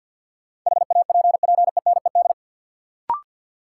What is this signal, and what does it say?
Morse code, heard as a keyed tone, about 53 wpm, 700 Hz. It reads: HAPPENED
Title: Sentences from Top 1000 Words - Review Part 2 - New Word Drill - 50wpm